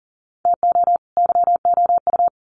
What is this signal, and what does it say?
Morse code, heard as a keyed tone, about 40 wpm, 700 Hz. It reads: TO YOU
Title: Sets of 2 Words 40wpm